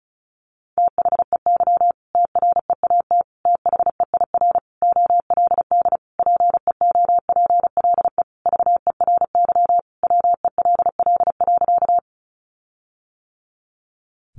Morse code, heard as a keyed tone, about 34 wpm, 700 Hz. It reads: THEY TREAT THEIR OLD PEOPLE VERY WELL.
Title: Sentences from Top 1300 Words 35wpm